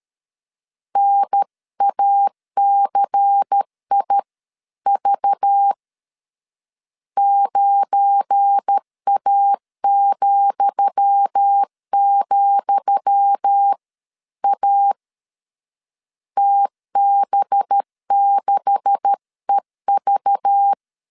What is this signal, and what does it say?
Morse code, heard as a keyed tone, about 13 wpm, 800 Hz. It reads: NACIV 9A,,A TB6EV
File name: LEZIONI / 10 SIGNS / 10D COMA 12wpm 2544